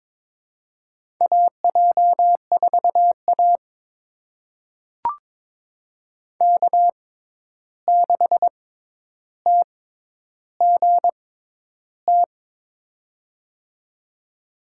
Morse code, heard as a keyed tone, about 22 wpm, spaced as for 11 wpm, 700 Hz. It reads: AJ4A K6TGT
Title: USA Call Signs - 6x Character Spacing - 22wpm